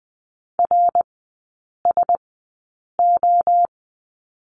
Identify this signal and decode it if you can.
Morse code, heard as a keyed tone, about 20 wpm, spaced as for 8 wpm, 700 Hz. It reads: RSO
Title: 053 Lesson 10e - Letter r - Three Characters 20wpm